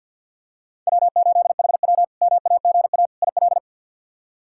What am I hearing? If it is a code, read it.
Morse code, heard as a keyed tone, about 52 wpm, 700 Hz. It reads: W8HW MAGA IL
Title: K1USN SST QSO Practice - 50wpm